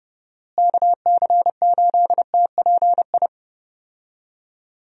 Morse code, heard as a keyed tone, about 30 wpm, 700 Hz. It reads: KC8TPI
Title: USA Call Signs - 6 Characters - Repeated 2x - 30wpm